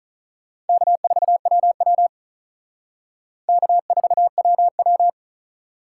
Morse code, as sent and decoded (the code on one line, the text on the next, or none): -.- ....- .-- .-- / -.- ....- .-- .--
K4WW K4WW